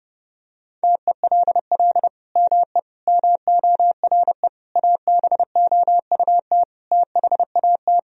Morse code, heard as a keyed tone, about 32 wpm, 700 Hz. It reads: TELL ME MORE ABOUT THAT
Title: Sentences from Top 1000 Words - Review Part 4 - 30wpm